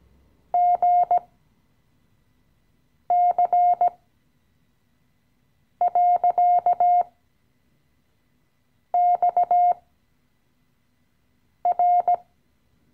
Morse code, heard as a keyed tone, about 17 wpm, spaced as for 4 wpm, 710 Hz. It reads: GC.XR
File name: CW-Online-Kurs / Anfänger / Lektion 015